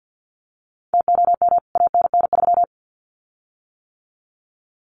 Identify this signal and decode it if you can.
Morse code, heard as a keyed tone, about 49 wpm, 700 Hz. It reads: TOM ANN3
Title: CWT Exchange 50wpm